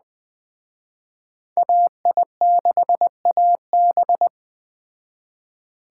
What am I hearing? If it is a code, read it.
Morse code, heard as a keyed tone, about 21 wpm, 700 Hz. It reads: AI6AB
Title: Speed Racing - USA 2x2 Call Signs 17wpm